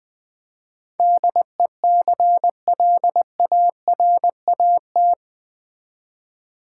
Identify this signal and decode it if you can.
Morse code, heard as a keyed tone, about 20 wpm, 700 Hz. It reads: DECLARAT
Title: Build A Word - 4 Letters At A Time - 11-Letter Words - 20wpm